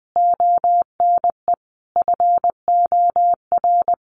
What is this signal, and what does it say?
Morse code, heard as a keyed tone, about 20 wpm, 700 Hz. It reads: ONE FOR
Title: Sentences from Top 1500 Words 20wpm